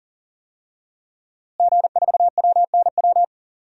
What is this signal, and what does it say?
Morse code, heard as a keyed tone, about 40 wpm, 700 Hz. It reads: G4WNW
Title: DXCC-100 Call Signs - 5 Characters - Repeated 2x - 40wpm